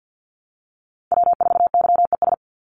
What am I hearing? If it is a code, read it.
Morse code, heard as a keyed tone, about 50 wpm, 700 Hz. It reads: W4YES
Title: Call Signs - 50wpm